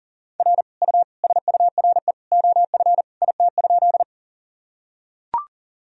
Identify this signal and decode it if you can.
Morse code, heard as a keyed tone, about 37 wpm, 700 Hz. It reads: R U SURE OF IT?
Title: Sentences from Top 300 Words and Abbreviations 40wpm